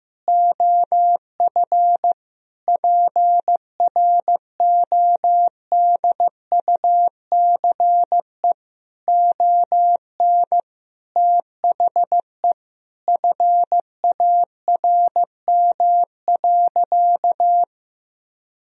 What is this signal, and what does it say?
Morse code, heard as a keyed tone, about 15 wpm, 700 Hz. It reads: OF PRODUCE ON THE FARM.
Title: Sentences from Top 1000 Words - Review Part 3 - 15wpm